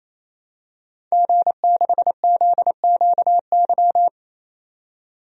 Morse code, heard as a keyed tone, about 27 wpm, 700 Hz. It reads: G6ZQY
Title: DXCC-100 Call Signs - 5 Characters - 28wpm